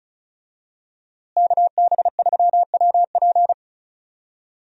Morse code, heard as a keyed tone, about 36 wpm, 700 Hz. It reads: KB3WP
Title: USA Call Signs - 5 Characters - Repeated 2x - 35wpm